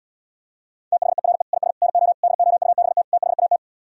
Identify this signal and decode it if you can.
Morse code, heard as a keyed tone, about 77 wpm, 700 Hz. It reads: THERE IS NO DOUBT THAT